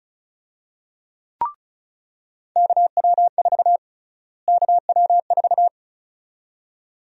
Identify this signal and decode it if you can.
Morse code, heard as a keyed tone, about 34 wpm, 700 Hz. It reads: KW4 KW4
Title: USA Call Sign Prefixes - Repeated 2x - 35wpm